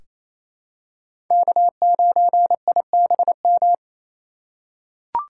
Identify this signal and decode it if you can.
Morse code, heard as a keyed tone, about 27 wpm, 700 Hz. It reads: K9IBM
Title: K1USN SST QSO Practice - 28wpm